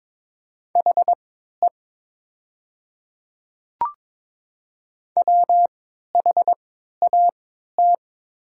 Morse code, heard as a keyed tone, about 22 wpm, spaced as for 12 wpm, 700 Hz. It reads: HE WHAT
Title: 3x Character Spacing - Top 100 Words - 22wpm